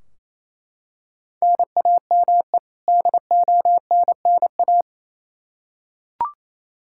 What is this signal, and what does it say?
Morse code, heard as a keyed tone, about 28 wpm, 700 Hz. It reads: NAME DONNA
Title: Practice QSO - Exchange 1 - Beginning - 28wpm